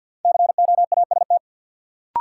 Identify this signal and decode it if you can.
Morse code, heard as a keyed tone, about 51 wpm, 700 Hz. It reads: COAST
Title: Sentences from Top 1000 Words - Review Part 4 - New Word Drill - 50wpm